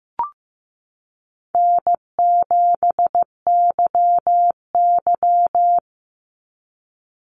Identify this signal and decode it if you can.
Morse code, heard as a keyed tone, about 15 wpm, 700 Hz. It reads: N7YY
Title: Easy Call Signs Repeated 3x - 15wpm